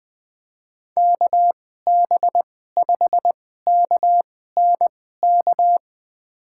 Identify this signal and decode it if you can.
Morse code, heard as a keyed tone, about 21 wpm, spaced as for 15 wpm, 700 Hz. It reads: KB5KNK
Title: USA Call Signs - 2x Character Spacing - 20wpm